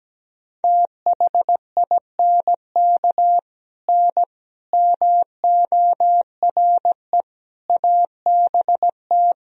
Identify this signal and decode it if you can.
Morse code, heard as a keyed tone, about 17 wpm, 700 Hz. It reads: THINK N MORE ABT
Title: Sentences from Top 200 Words and Abbreviations 17wpm